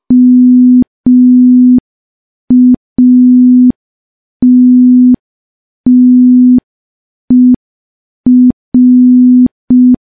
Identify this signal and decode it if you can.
Morse code, heard as a keyed tone, about 5 wpm, 250 Hz. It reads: MATTER
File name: morse / morse-training-05WPM